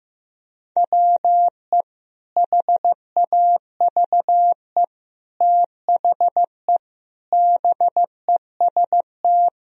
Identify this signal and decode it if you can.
Morse code, heard as a keyed tone, about 15 wpm, 700 Hz. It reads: WE HAVE THE BEST
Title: Sentences from Top 1000 Words - Review Part 4 - 15wpm